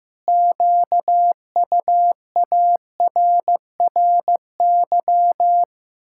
Morse code, heard as a keyed tone, about 15 wpm, 700 Hz. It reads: QUARRY
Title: Pangram Challenge v2 - 15wpm